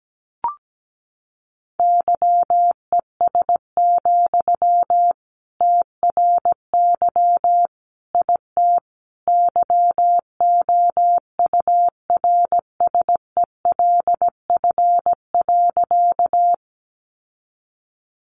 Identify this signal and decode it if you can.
Morse code, heard as a keyed tone, about 17 wpm, 700 Hz. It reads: YES, TRY IT YOURSELF.
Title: Sentences from Top 600 Words 17wpm